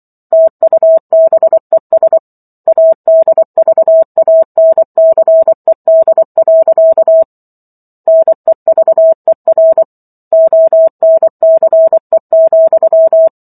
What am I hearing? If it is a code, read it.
Morse code, heard as a keyed tone, about 24 wpm, 650 Hz. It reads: TUBES ADVANCED. NEVER ONCE,